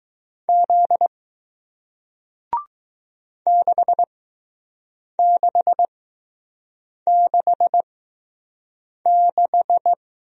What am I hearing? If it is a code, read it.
Morse code, heard as a keyed tone, about 20 wpm, spaced as for 6 wpm, 700 Hz. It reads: Z 6666